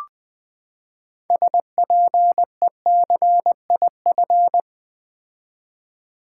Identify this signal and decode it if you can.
Morse code, heard as a keyed tone, about 20 wpm, 700 Hz. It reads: SPECIF
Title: Build A Word - 6 Letters At A Time - 12-Letter Words - 20wpm